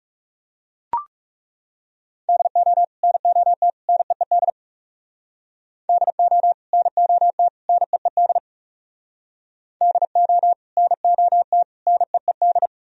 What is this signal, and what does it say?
Morse code, heard as a keyed tone, about 37 wpm, 700 Hz. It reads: DO NOT NEED DO NOT NEED DO NOT NEED
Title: Speed-Racing - Sets of 3 Words 30wpm